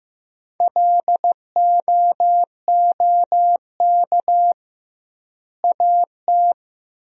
Morse code, heard as a keyed tone, about 15 wpm, 700 Hz. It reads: LOOK AT
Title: 2x Word-Spacing - Sentences from Top 100 Words - 15wpm